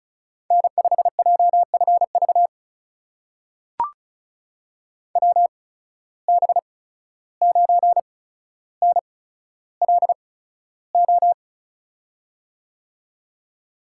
Morse code, heard as a keyed tone, about 34 wpm, spaced as for 14 wpm, 700 Hz. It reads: N5JFV WB9NLO
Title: USA Call Signs - 8x Character Spacing - 35wpm